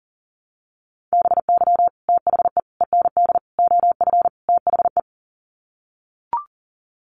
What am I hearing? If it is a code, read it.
Morse code, heard as a keyed tone, about 40 wpm, 700 Hz. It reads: BY THE END OF THE
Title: Sets of 5 Words 40wpm